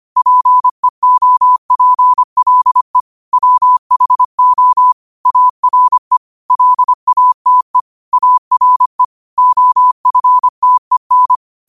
Morse code, heard as a keyed tone, about 25 wpm, 1000 Hz. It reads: PEOPLE WHO ARE LATE ARE OFTEN
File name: qotdfiles / 25wpm